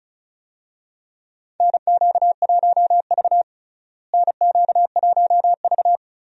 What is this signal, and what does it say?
Morse code, heard as a keyed tone, about 36 wpm, 700 Hz. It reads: NQ1V NQ1V
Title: USA Call Signs - 4 Characters - Repeated 3x - 35wpm